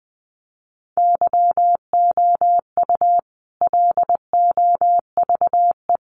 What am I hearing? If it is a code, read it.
Morse code, heard as a keyed tone, about 19 wpm, 700 Hz. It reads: YOU LOVE
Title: Sentences from Top 400 Words 20wpm